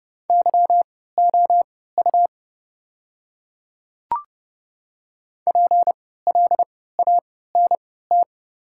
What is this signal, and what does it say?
Morse code, heard as a keyed tone, about 30 wpm, spaced as for 17 wpm, 700 Hz. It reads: YOU PLANT
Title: 3x Character Spacing - Top 300 Words - 30wpm